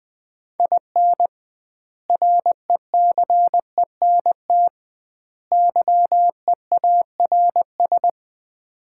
Morse code, heard as a keyed tone, about 20 wpm, 700 Hz. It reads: IN RECENT YEARS
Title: 2x Word-Spacing - Sets of 3 Words - 20wpm